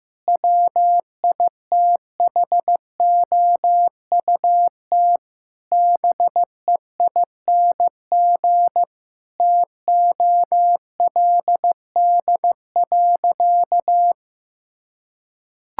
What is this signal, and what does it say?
Morse code, heard as a keyed tone, about 16 wpm, 700 Hz. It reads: WITHOUT BEING TOLD.
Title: Sentences from Top 1000 Words - Review Part 6 - 15wpm